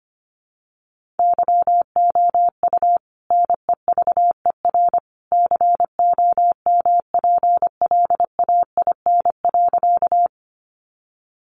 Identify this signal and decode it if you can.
Morse code, heard as a keyed tone, about 25 wpm, 700 Hz. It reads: YOU NEVER COMPLAIN.
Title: Sentences from Top 1300 Words 25wpm